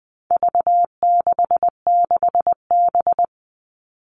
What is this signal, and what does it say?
Morse code, heard as a keyed tone, about 20 wpm, 700 Hz. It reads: V66B
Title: All Cut Numbers - 4-Digit 20wpm